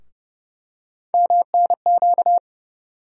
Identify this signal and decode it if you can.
Morse code, heard as a keyed tone, about 31 wpm, 700 Hz. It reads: MNQ
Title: USA Call Sign Postfixes - 3 Letters 30wpm